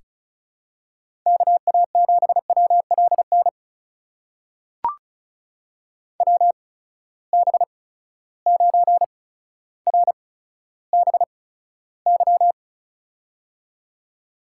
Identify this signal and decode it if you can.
Morse code, heard as a keyed tone, about 36 wpm, spaced as for 15 wpm, 700 Hz. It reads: KA7WLN WB9RBY
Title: USA Call Signs - 6 Characters - 8x Character Spacing - 35wpm